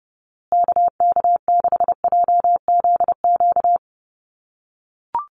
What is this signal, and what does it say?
Morse code, heard as a keyed tone, about 29 wpm, 700 Hz. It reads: KK6JZQ
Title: USA Call Signs Repeated 2x 30wpm